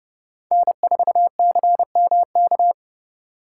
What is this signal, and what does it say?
Morse code, heard as a keyed tone, about 32 wpm, 700 Hz. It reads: N4CMK